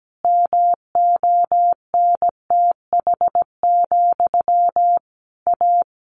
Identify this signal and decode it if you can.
Morse code, heard as a keyed tone, about 17 wpm, 700 Hz. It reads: MONTH, A
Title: A Sound Of Thunder 17wpm